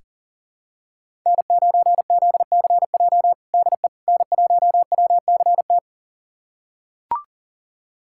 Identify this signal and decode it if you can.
Morse code, heard as a keyed tone, about 40 wpm, 700 Hz. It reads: N9ZCJ DE N1WCT
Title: USA Callsign DE Callsign - 5 Characters - 40wpm